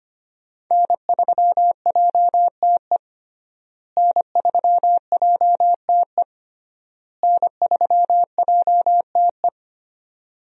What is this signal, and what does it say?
Morse code, heard as a keyed tone, about 25 wpm, 700 Hz. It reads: N3JTE N3JTE N3JTE
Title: USA Call Signs - 5 Characters - Repeated 3x - 25wpm